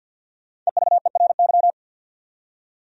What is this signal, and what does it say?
Morse code, heard as a keyed tone, about 65 wpm, spaced as for 57 wpm, 700 Hz. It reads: EVERY
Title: Speed-Racing Top 200 Words 50wpm